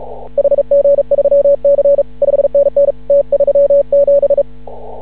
Morse code, heard as a keyed tone, about 36 wpm, 560 Hz. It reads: HG3C 5NN T37
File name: CW / ha6oi 2